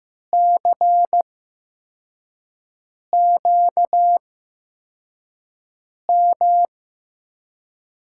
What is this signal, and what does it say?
Morse code, heard as a keyed tone, about 15 wpm, spaced as for 4 wpm, 700 Hz. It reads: CQM
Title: USA Call Signs - 8x Character Spacing - 15wpm